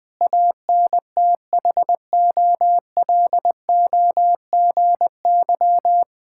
Morse code, heard as a keyed tone, about 20 wpm, 700 Hz. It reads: ANTHOLOGY
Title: Difficult Words with Hints - 20wpm